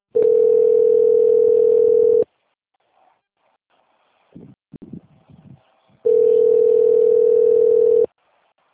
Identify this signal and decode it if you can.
Ringback tone (North America), 440 + 480 Hz.
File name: out- - - - - .64887